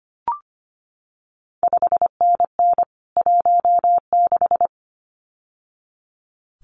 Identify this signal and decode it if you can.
Morse code, heard as a keyed tone, about 25 wpm, 700 Hz. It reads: 5NN 16
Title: CQ WW DX Contest Exchange - 25wpm